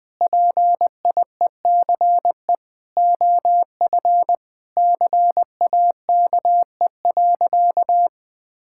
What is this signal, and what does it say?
Morse code, heard as a keyed tone, about 20 wpm, 700 Hz. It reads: PIECE OF CAKE.
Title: Sentences with 3-Word Phrases - 20wpm